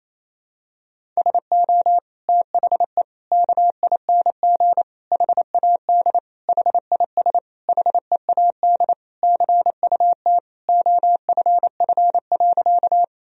Morse code, heard as a keyed tone, about 28 wpm, 700 Hz. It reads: SO THE KING HAD HIS HEAD CUT OFF.
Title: Sentences from Top 1000 Words - Review Part 1 - 28wpm